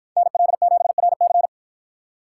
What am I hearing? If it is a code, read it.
Morse code, heard as a keyed tone, about 50 wpm, 700 Hz. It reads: NL7RC